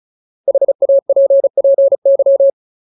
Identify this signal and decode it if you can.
Morse code, heard as a keyed tone, about 34 wpm, 550 Hz. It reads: HAPPY
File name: Ultimate Word List / 35WPM Ultimate